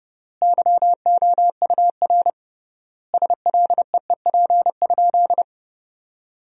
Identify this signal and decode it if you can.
Morse code, heard as a keyed tone, about 30 wpm, 700 Hz. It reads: YOUR SLEEP?
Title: 3x Word-Spacing - Sentences from Top 500 Words - 30wpm